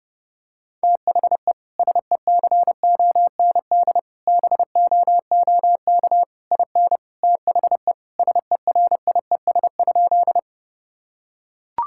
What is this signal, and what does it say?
Morse code, heard as a keyed tone, about 30 wpm, 700 Hz. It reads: THE SECOND BOOK IN THE SERIES?
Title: Sentences from Top 1000 Words - Review Part 5 - 30wpm